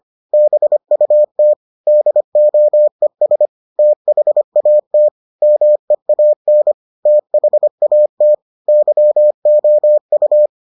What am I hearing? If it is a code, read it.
Morse code, heard as a keyed tone, about 25 wpm, 600 Hz. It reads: BUT DOES THAT MEAN THAT YOU